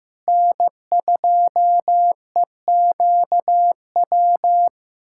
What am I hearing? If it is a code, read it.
Morse code, heard as a keyed tone, about 15 wpm, 700 Hz. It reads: N2EQW
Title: USA Callsign DE Callsign - 5 Characters - 15wpm